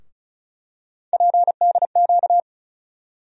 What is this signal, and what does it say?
Morse code, heard as a keyed tone, about 36 wpm, 700 Hz. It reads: PDQ
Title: USA Call Sign Postfixes - 3 Letters 35wpm